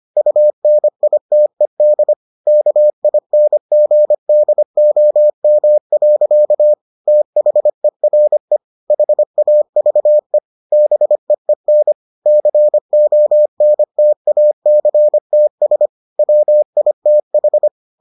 Morse code, heard as a keyed tone, about 25 wpm, 600 Hz. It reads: UNITED KINGDOM. THERE HAVE BEEN CONTACTS WITH